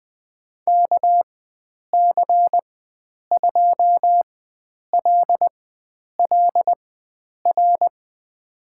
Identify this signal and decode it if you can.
Morse code, heard as a keyed tone, about 21 wpm, spaced as for 9 wpm, 700 Hz. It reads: KC2LLR
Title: USA Call Signs - 6 Characters - 4x Character Spacing - 20wpm